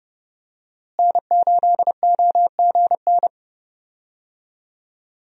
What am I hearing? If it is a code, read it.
Morse code, heard as a keyed tone, about 31 wpm, 700 Hz. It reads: N8OGN
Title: USA Call Signs - 5 Characters - Repeated 2x - 30wpm